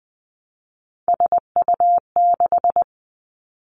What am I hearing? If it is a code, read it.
Morse code, heard as a keyed tone, about 20 wpm, 700 Hz. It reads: SU6
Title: Three Challenging Letter-Number Combinations 20wpm